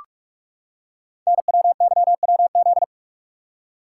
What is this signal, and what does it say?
Morse code, heard as a keyed tone, about 44 wpm, 690 Hz. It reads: NWYWZ
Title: Code Groups - 5 Letters - 45wpm